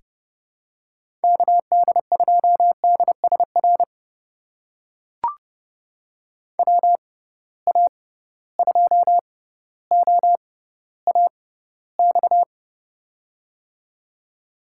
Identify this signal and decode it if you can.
Morse code, heard as a keyed tone, about 30 wpm, spaced as for 15 wpm, 700 Hz. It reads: KD2DSR WA2OAX
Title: USA Call Signs - 6 Characters - 6x Character Spacing - 30wpm